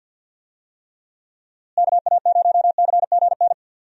Morse code, heard as a keyed tone, about 46 wpm, 700 Hz. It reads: KA0CGN
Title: USA Call Signs - 6 Characters - Repeated 3x - 50wpm